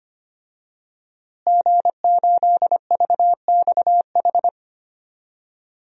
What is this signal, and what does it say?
Morse code, heard as a keyed tone, about 25 wpm, 700 Hz. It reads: G8VXH
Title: DXCC-100 Call Signs - 5 Characters - Repeated 2x - 25wpm